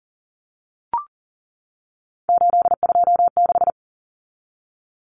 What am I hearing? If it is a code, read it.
Morse code, heard as a keyed tone, about 40 wpm, 700 Hz. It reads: 826
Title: Common Cut Numbers - 3-Digit 40wpm